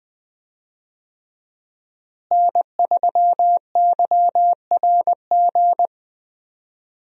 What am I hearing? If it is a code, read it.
Morse code, heard as a keyed tone, about 20 wpm, 700 Hz. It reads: N3YRG
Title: USA Call Signs - 5 Characters - Repeated 3x - 20wpm